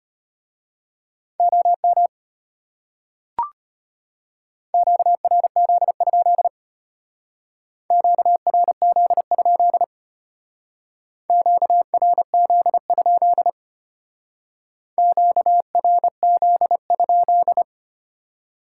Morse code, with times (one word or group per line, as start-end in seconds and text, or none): 1.4-2.1 OM
4.7-6.5 QRZ?
7.9-9.8 QRZ?
11.3-13.5 QRZ?
15.0-17.6 QRZ?